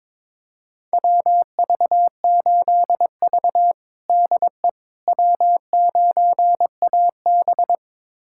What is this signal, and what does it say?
Morse code, heard as a keyed tone, about 22 wpm, 700 Hz. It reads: WV8V DE W9AB
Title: USA Callsign DE Callsign - 4 Characters - 22wpm